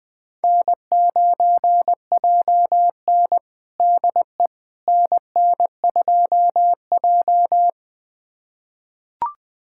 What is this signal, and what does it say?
Morse code, heard as a keyed tone, about 20 wpm, 700 Hz. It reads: N9JN DE NN2J
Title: USA Callsign DE Callsign - 4 Characters - 20wpm